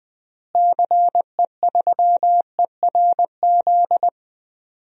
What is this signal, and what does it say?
Morse code, heard as a keyed tone, about 20 wpm, 700 Hz. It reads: CE3ERZ
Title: DXCC-100 Call Signs - 6 Characters - Repeated 3x - 20wpm